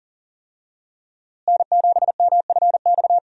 Morse code, heard as a keyed tone, about 40 wpm, 700 Hz. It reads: N7MFX
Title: USA Call Signs - 5 Characters - Repeated 2x - 40wpm